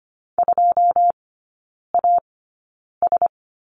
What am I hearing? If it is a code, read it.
Morse code, heard as a keyed tone, about 25 wpm, spaced as for 9 wpm, 690 Hz. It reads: 2AS
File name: Lesson 15e - Number 5 - Three Characters 25wpm